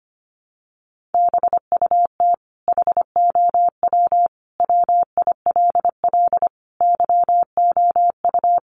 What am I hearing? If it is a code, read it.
Morse code, heard as a keyed tone, about 25 wpm, 700 Hz. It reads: BUT HOW WILL YOU
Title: Sentences from Top 1400 Words 25wpm